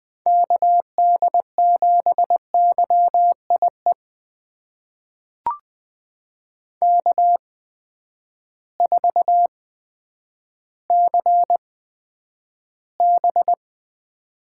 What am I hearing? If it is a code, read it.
Morse code, heard as a keyed tone, about 20 wpm, spaced as for 10 wpm, 700 Hz. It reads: KD7YIE K4CB
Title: USA Call Signs - 8x Character Spacing - 20wpm